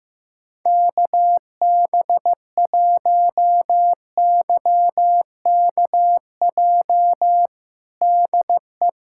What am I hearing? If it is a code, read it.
Morse code, heard as a keyed tone, about 16 wpm, 700 Hz. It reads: KB1YKJ DE
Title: USA Callsign DE Callsign - 6 Characters - 15wpm